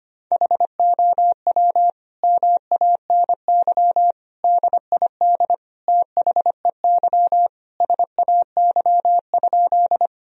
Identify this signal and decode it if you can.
Morse code, heard as a keyed tone, about 25 wpm, 700 Hz. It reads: HOW MANY DID THEY SAY?
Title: Sentences from Top 1000 Words - Review Part 1 - 25wpm